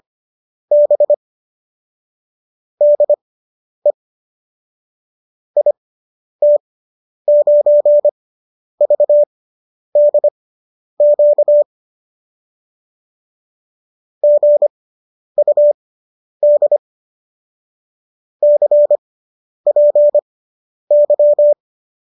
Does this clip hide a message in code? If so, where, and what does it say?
Yes: Morse code from 0.7 to 21.5 s, heard as a keyed tone, about 25 wpm, spaced as for 10 wpm, 590 Hz. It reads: B DE IT9VDQ GUD CPY